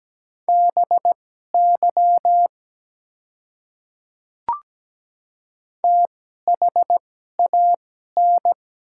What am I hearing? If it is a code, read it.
Morse code, heard as a keyed tone, about 17 wpm, spaced as for 12 wpm, 690 Hz. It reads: BY THAN
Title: 2x Character Spacing - Top 300 Words - 17wpm